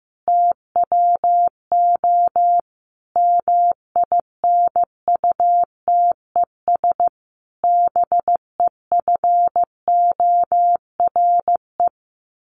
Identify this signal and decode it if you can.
Morse code, heard as a keyed tone, about 15 wpm, 700 Hz. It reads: TWO MINUTES BEFORE
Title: A Sound Of Thunder 15wpm